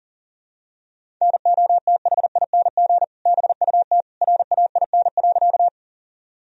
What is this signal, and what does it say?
Morse code, heard as a keyed tone, about 37 wpm, 700 Hz. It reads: NOTHING BUT RAIN.
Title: Sentences from Top 1000 Words - Review Part 1 - 40wpm